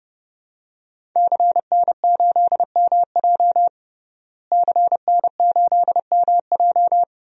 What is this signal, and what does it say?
Morse code, heard as a keyed tone, about 30 wpm, 700 Hz. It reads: CN8MJ CN8MJ
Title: DXCC-100 Call Signs - 5 Characters - Repeated 2x - 30wpm